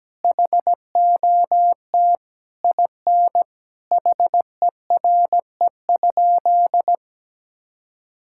Morse code, heard as a keyed tone, about 17 wpm, 700 Hz. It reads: HOT IN HERE?
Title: Sentences from Top 1000 Words - Review Part 6 - 17wpm